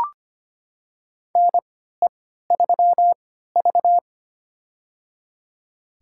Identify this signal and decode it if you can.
Morse code, heard as a keyed tone, about 25 wpm, spaced as for 14 wpm, 700 Hz. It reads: NE3V
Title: USA Call Signs - 4 Characters - 3x Character Spacing - 25wpm